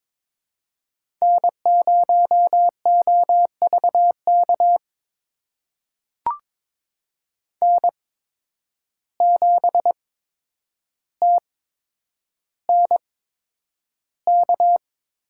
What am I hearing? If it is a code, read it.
Morse code, heard as a keyed tone, about 22 wpm, spaced as for 9 wpm, 700 Hz. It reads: N0OVK N7TNK
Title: USA Call Signs - 5 Characters - 8x Character Spacing - 22wpm